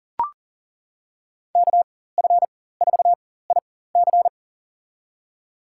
Morse code, heard as a keyed tone, about 40 wpm, spaced as for 19 wpm, 700 Hz. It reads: KF4IC
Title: USA Call Signs - 4x Character Spacing - 40wpm